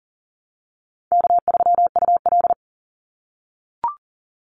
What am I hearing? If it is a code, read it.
Morse code, heard as a keyed tone, about 40 wpm, 700 Hz. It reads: K3UL
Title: Easy Call Signs - 40wpm